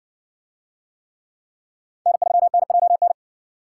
Morse code, heard as a keyed tone, about 54 wpm, 700 Hz. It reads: N3NJN